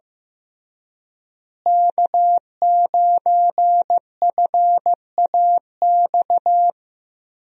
Morse code, heard as a keyed tone, about 15 wpm, 700 Hz. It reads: K9FAX
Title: USA Call Signs - 5 Characters - Repeated 3x - 15wpm